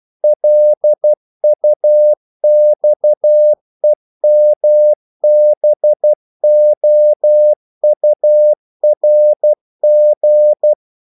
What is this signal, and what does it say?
Morse code, heard as a keyed tone, about 12 wpm, 600 Hz. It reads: LUXEMBOURG
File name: CW / Practice / NationalCapitals12